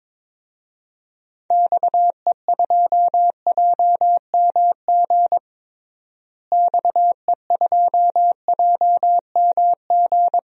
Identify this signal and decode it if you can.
Morse code, heard as a keyed tone, about 22 wpm, 700 Hz. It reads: XE2JMG XE2JMG
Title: DXCC-100 Call Signs - 6 Characters - Repeated 2x - 22wpm